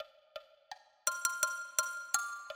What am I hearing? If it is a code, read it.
There is no code.